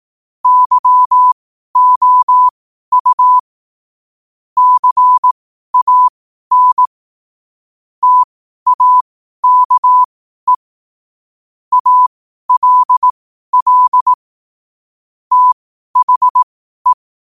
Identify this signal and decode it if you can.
Morse code, heard as a keyed tone, about 18 wpm, spaced as for 12 wpm, 1000 Hz. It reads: YOU CAN TAKE ALL THE
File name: qotdfiles / 12wpm